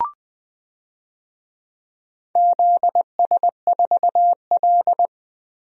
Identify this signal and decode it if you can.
Morse code, heard as a keyed tone, about 20 wpm, 700 Hz. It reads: ZS4L
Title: DXCC-100 Call Signs - 4 Characters - Repeated 2x - 20wpm